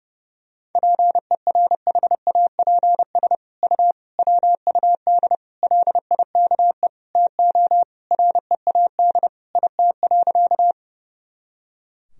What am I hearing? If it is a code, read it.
Morse code, heard as a keyed tone, about 30 wpm, 700 Hz. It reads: PERHAPS U WUD LIKE TO READ IT.
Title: Sentences from Top 500 Words and Abbreviations 30wpm